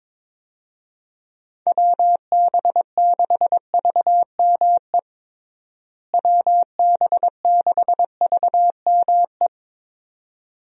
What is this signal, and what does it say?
Morse code, heard as a keyed tone, about 21 wpm, 700 Hz. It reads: WB6VME WB6VME
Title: USA Call Signs - 6 Characters - Repeated 2x - 22wpm